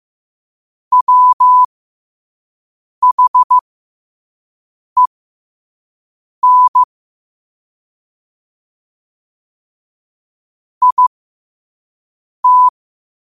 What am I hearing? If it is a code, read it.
Morse code, heard as a keyed tone, about 15 wpm, spaced as for 5 wpm, 1000 Hz. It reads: WHEN IT
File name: qotdfiles / 5wpm